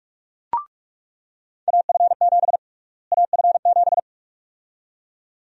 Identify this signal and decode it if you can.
Morse code, heard as a keyed tone, about 42 wpm, 700 Hz. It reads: AF7 AF7
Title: USA Call Sign Prefixes - Repeated 2x - 45wpm